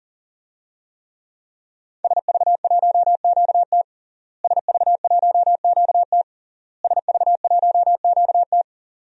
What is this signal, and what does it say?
Morse code, heard as a keyed tone, about 40 wpm, 700 Hz. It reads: SV1QT SV1QT SV1QT
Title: DXCC-100 Call Signs - 5 Characters - Repeated 3x - 40wpm